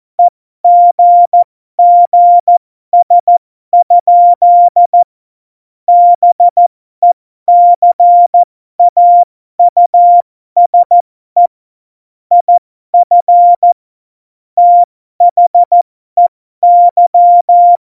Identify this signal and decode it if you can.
Morse code, heard as a keyed tone, about 14 wpm, spaced as for 12 wpm, 700 Hz. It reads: EGGS? BECAUSE IF THEY
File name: Morsekurs / Witze